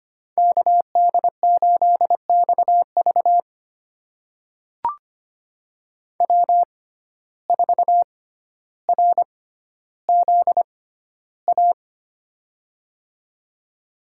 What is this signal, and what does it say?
Morse code, heard as a keyed tone, about 25 wpm, spaced as for 13 wpm, 700 Hz. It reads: KD8XV W4RZA
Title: USA Call Signs - 5 Characters - 6x Character Spacing - 25wpm